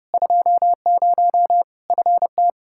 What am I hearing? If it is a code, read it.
Morse code, heard as a keyed tone, about 29 wpm, 700 Hz. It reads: 20 FT
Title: Practice QSO - Exchange 2 - Info - 30wpm